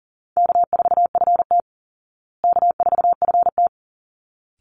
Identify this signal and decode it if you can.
Morse code, heard as a keyed tone, about 40 wpm, 700 Hz. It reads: K4FT K4FT
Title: Easy Call Signs Repeated 2x - 40wpm